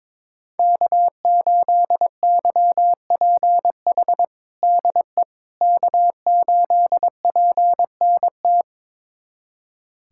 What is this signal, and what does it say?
Morse code, heard as a keyed tone, about 21 wpm, 700 Hz. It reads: K8YPH DE K8PNT
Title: USA Callsign DE Callsign - 5 Characters - 22wpm